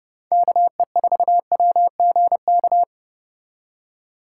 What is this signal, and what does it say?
Morse code, heard as a keyed tone, about 30 wpm, 700 Hz. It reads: KE4WGK
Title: USA Call Signs - 6 Characters - 4x Character Spacing - 30wpm